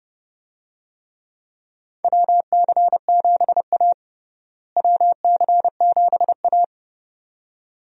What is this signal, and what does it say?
Morse code, heard as a keyed tone, about 30 wpm, 700 Hz. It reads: WC7A WC7A
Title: USA Call Signs - 4 Characters - Repeated 2x - 30wpm